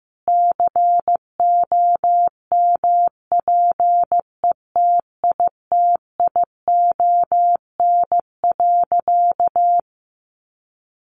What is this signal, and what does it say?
Morse code, heard as a keyed tone, about 15 wpm, 700 Hz. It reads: COMPETITION.
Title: Additional Easy Phrases 15wpm